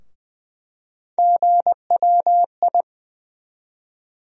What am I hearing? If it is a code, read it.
Morse code, heard as a keyed tone, about 21 wpm, 700 Hz. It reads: GWI
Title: USA Call Sign Postfixes - 3 Letters 20wpm